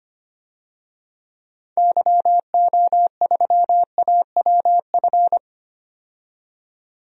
Morse code, heard as a keyed tone, about 25 wpm, 700 Hz. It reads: YO3AWF
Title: DXCC-100 Call Signs - 6 Characters - 25wpm